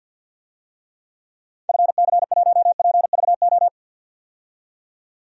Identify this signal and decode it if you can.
Morse code, heard as a keyed tone, about 52 wpm, 700 Hz. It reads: FC1PVO
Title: DXCC-100 Call Signs - 6 Characters - 50wpm